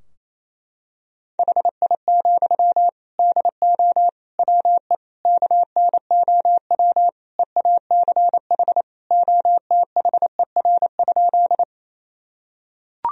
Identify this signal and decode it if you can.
Morse code, heard as a keyed tone, about 28 wpm, 700 Hz. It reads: HI, DO WE KNOW EACH OTHER?
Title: Sentences from Top 1000 Words - Review Part 3 - 28wpm